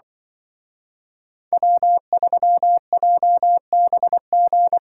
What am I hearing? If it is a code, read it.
Morse code, heard as a keyed tone, about 24 wpm, 700 Hz. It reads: W3JBG
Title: Speed Racing - USA 1x3 Call Signs 20wpm